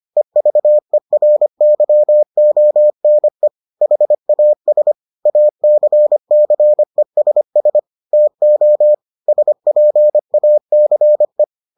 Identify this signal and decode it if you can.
Morse code, heard as a keyed tone, about 25 wpm, 600 Hz. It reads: EVERYONE HAS ACCESS TO SPACE